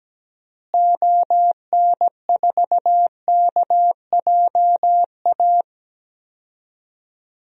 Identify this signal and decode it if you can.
Morse code, heard as a keyed tone, about 17 wpm, 700 Hz. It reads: ON4KJA
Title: DXCC-100 Call Signs - 6 Characters - Repeated 2x - 17wpm